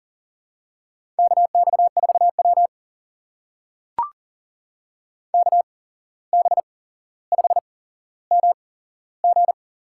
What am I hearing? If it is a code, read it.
Morse code, heard as a keyed tone, about 41 wpm, spaced as for 16 wpm, 700 Hz. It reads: KX4W KB5MG
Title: USA Call Signs - 8x Character Spacing - 40wpm